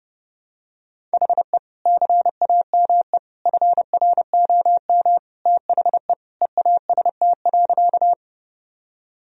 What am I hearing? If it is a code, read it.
Morse code, heard as a keyed tone, about 28 wpm, 700 Hz. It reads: HE CAME FROM THE EAST.
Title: Sentences from Top 1000 Words - Review Part 3 - 30wpm